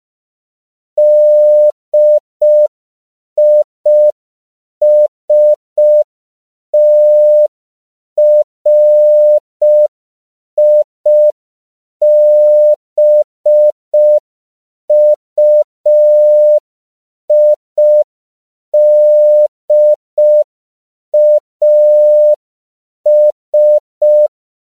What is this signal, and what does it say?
Morse code, heard as a keyed tone, about 5 wpm, 600 Hz. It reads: DISTRIBUIDAS
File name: CW / 5PPM